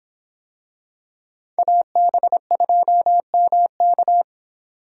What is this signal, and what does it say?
Morse code, heard as a keyed tone, about 26 wpm, 700 Hz. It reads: AB2MK